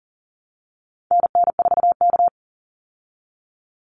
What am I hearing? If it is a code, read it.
Morse code, heard as a keyed tone, about 39 wpm, 700 Hz. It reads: NN4K
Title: Call Signs Repeated 2x - 40wpm